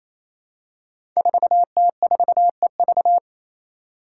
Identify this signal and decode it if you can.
Morse code, heard as a keyed tone, about 28 wpm, 700 Hz. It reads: 4T4EV
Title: Five Challenging Letter-Number Combinations 28wpm